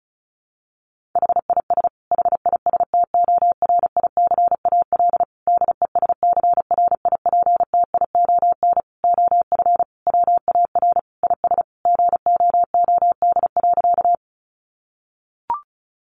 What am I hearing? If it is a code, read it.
Morse code, heard as a keyed tone, about 34 wpm, 700 Hz. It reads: HIS HISTORICAL DESCRIPTION OF WAR IS GOOD.
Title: Sentences from Top 1300 Words 35wpm